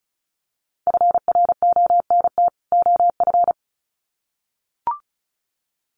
Morse code, heard as a keyed tone, about 34 wpm, 700 Hz. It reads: FRONT OF
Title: Sets of 2 Words 35wpm